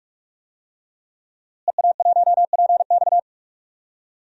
Morse code, heard as a keyed tone, about 44 wpm, 700 Hz. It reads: EA1PX
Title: DXCC-100 Call Signs - 5 Characters - Repeated 2x - 45wpm